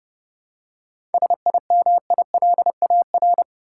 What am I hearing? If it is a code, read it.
Morse code, heard as a keyed tone, about 30 wpm, 700 Hz. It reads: SIMILAR
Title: Seven Letter Words - 30wpm